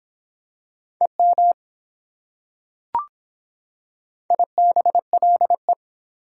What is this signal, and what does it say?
Morse code, heard as a keyed tone, about 24 wpm, 700 Hz. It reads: EM IBLE